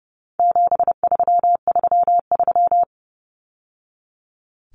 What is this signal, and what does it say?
Morse code, heard as a keyed tone, about 30 wpm, 700 Hz. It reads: 7333
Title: Numbers - 4-Digit 30wpm